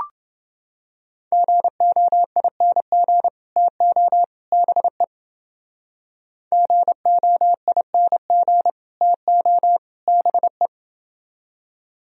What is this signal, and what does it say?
Morse code, heard as a keyed tone, about 29 wpm, 700 Hz. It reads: GOING TO BE GOING TO BE